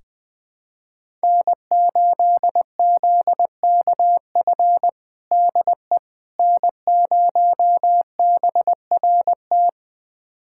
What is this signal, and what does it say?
Morse code, heard as a keyed tone, about 20 wpm, 700 Hz. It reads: N8ZKF DE N0BRT
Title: USA Callsign DE Callsign - 5 Characters - 20wpm